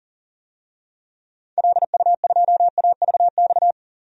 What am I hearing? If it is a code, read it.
Morse code, heard as a keyed tone, about 40 wpm, 700 Hz. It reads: LU2AVX
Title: DXCC-100 Call Signs - 6 Characters - 40wpm